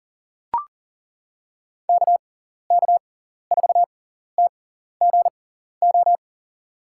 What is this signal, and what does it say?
Morse code, heard as a keyed tone, about 40 wpm, spaced as for 14 wpm, 700 Hz. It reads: KK4TGO
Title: USA Call Signs - 6 Characters - 6x Character Spacing - 40wpm